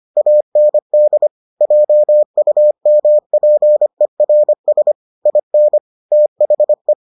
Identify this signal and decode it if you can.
Morse code, heard as a keyed tone, about 25 wpm, 600 Hz. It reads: AND JUMPERS IN THE